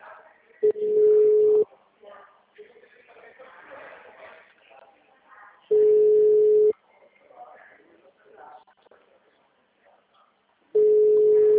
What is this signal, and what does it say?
Ringback tone (Europe, 425 Hz), with speech.
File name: out- -2014- - - .6263